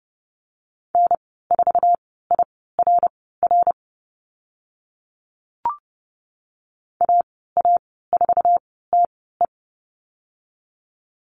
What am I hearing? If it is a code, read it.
Morse code, heard as a keyed tone, about 30 wpm, spaced as for 17 wpm, 700 Hz. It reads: N4IRR AA4TE
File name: Lesson 15g - Number 5 - Call Signs 30wpm